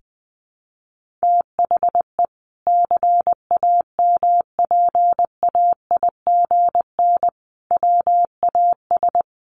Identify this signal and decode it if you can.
Morse code, heard as a keyed tone, about 20 wpm, 700 Hz. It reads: THE CAMPAIGN WAS
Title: Sentences from Top 1400 Words 20wpm